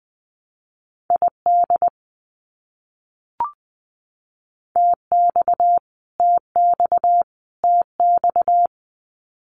Morse code, heard as a keyed tone, about 20 wpm, 700 Hz. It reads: ID TX TX TX
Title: US States Repeated 3x 20wpm